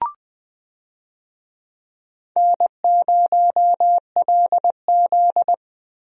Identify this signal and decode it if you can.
Morse code, heard as a keyed tone, about 20 wpm, 700 Hz. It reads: N0LZ
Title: USA Call Signs - 4 Characters - Repeated 2x - 20wpm